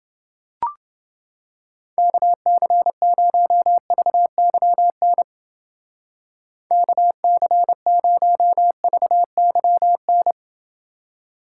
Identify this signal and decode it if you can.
Morse code, heard as a keyed tone, about 31 wpm, 700 Hz. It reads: KC0VYN KC0VYN